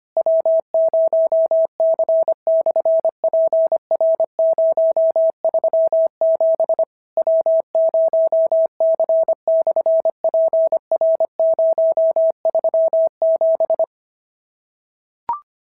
Morse code, heard as a keyed tone, about 25 wpm, 650 Hz. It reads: W0C/PR037 W0C/PR037
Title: SOTA QSO Practice 25wpm